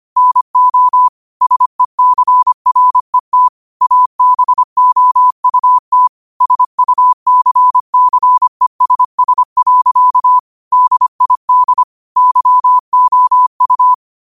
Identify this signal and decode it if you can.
Morse code, heard as a keyed tone, about 25 wpm, 1000 Hz. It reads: NO SECRET ABOUT SUCCESS. DID YOU
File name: qotdfiles / 25wpm